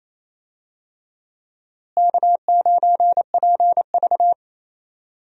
Morse code, heard as a keyed tone, about 27 wpm, 700 Hz. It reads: K9PV
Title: USA Call Signs - 4 Characters - Repeated 2x - 28wpm